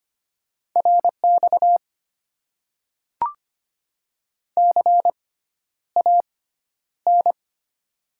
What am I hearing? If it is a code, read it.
Morse code, heard as a keyed tone, about 25 wpm, spaced as for 11 wpm, 690 Hz. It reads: RX CAN
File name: Top 90 QSO Elements - 6x Character Spacing - 25wpm